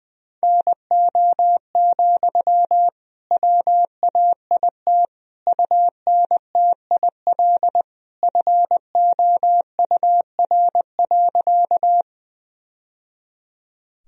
Morse code, heard as a keyed tone, about 20 wpm, 700 Hz. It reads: NO, WAIT UNTIL FOUR.
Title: Sentences from Top 1000 Words - Review Part 1 - 20wpm